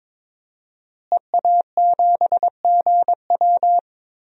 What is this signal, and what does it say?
Morse code, heard as a keyed tone, about 22 wpm, 700 Hz. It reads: EA7GW
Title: DXCC-100 Call Signs - 5 Characters - Repeated 2x - 22wpm